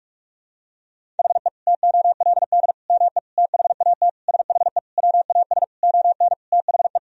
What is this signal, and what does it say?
Morse code, heard as a keyed tone, about 47 wpm, 700 Hz. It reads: HE TOLD ME THAT SHE WAS ON THE
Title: Sentences from Top 1000 Words - Review Part 2 - 45wpm